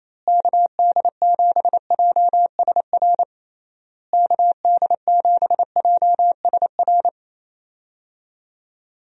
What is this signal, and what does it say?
Morse code, heard as a keyed tone, about 28 wpm, 700 Hz. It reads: KD7JSR KD7JSR
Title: USA Call Signs - 6 Characters - Repeated 3x - 28wpm